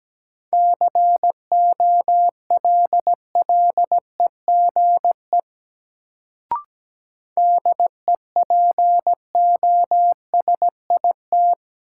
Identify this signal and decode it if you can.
Morse code, heard as a keyed tone, about 17 wpm, 700 Hz. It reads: COLLEGE DEPOSIT
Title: Sentences from Top 1300 Words - New Word Drill - 17wpm